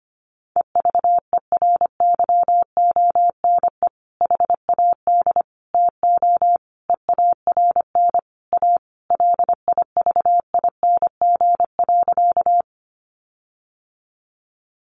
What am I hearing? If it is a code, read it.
Morse code, heard as a keyed tone, about 25 wpm, 700 Hz. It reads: EVERYONE HAD TO EARN A LIVING.
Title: Sentences from Top 1500 Words 25wpm